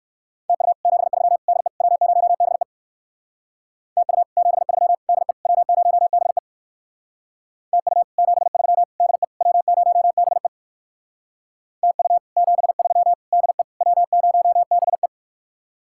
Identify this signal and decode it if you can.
Morse code, heard as a keyed tone, about 57 wpm, 700 Hz. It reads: TU 73 DE W0BE TU 73 DE W0BE TU 73 DE W0BE TU 73 DE W0BE